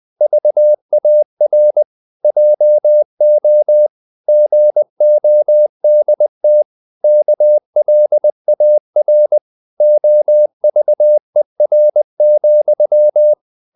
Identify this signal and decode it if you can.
Morse code, heard as a keyed tone, about 20 wpm, 600 Hz. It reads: VAR JO GODT KLAR OVER,
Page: Ferie 19 wpm